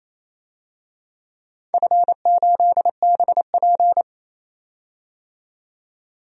Morse code, heard as a keyed tone, about 27 wpm, 700 Hz. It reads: F8BP
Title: DXCC-100 Call Signs - 4 Characters - 28wpm